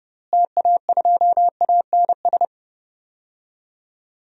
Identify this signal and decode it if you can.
Morse code, heard as a keyed tone, about 32 wpm, 700 Hz. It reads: TA2ANS
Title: DXCC-100 Call Signs - 6 Characters - Repeated 3x - 30wpm